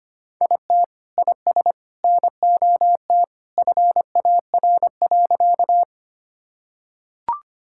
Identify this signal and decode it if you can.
Morse code, heard as a keyed tone, about 25 wpm, 700 Hz. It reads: IT IS NOT FAR.
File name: practice / 25wpm / Sentences with 3-Word Phrases - 25wpm